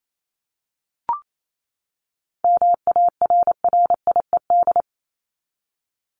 Morse code, heard as a keyed tone, about 28 wpm, 700 Hz. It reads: MARRIED
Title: ED Suffix - Top 100 Words with Base Words 28wpm